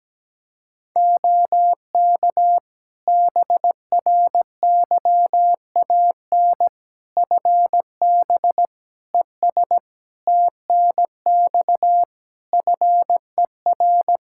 Morse code, heard as a keyed tone, about 17 wpm, 700 Hz. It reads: OK BRYAN FB ES TNX FER
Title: Practice QSO - Exchange 2 - Info - 17wpm